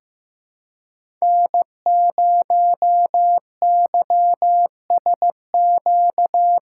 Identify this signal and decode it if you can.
Morse code, heard as a keyed tone, about 15 wpm, 700 Hz. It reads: N0YSQ
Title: K1USN SST QSO Practice - 15wpm